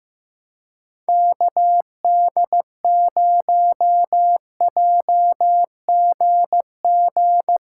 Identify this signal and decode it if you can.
Morse code, heard as a keyed tone, about 15 wpm, 700 Hz. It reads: KD0JGG
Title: USA Callsign DE Callsign 15wpm